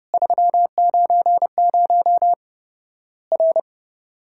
Morse code, heard as a keyed tone, about 30 wpm, 700 Hz. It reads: 390 R
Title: Worked All Europe DX Contest - QTC Exchange - 30wpm